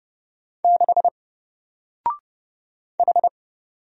Morse code, heard as a keyed tone, about 30 wpm, 700 Hz. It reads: 6 H
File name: Single-Letter-Number-30wpm